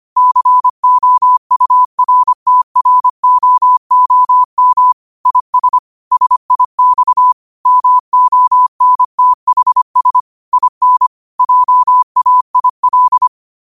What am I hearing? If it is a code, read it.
Morse code, heard as a keyed tone, about 25 wpm, 1000 Hz. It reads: COURTROOM IS SIX MONTHS IN JAIL
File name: qotdfiles / 25wpm